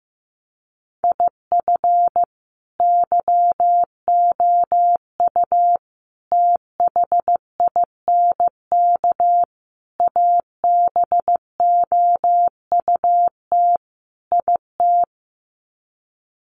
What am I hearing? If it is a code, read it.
Morse code, heard as a keyed tone, about 15 wpm, 700 Hz. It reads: IF YOU THINK ABOUT IT
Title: Sets of 5 Words Encore2 15wpm